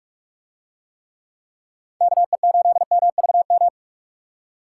Morse code, heard as a keyed tone, about 43 wpm, 700 Hz. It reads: KE8MVM